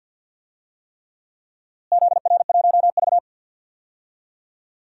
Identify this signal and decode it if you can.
Morse code, heard as a keyed tone, about 48 wpm, 700 Hz. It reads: ZR1V
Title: DXCC-100 Call Signs - 4 Characters - Repeated 3x - 50wpm